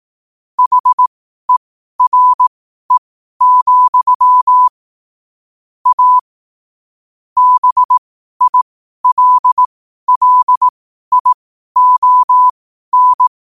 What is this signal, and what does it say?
Morse code, heard as a keyed tone, about 18 wpm, spaced as for 12 wpm, 1000 Hz. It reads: HERE, A BILLION
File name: qotdfiles / 12wpm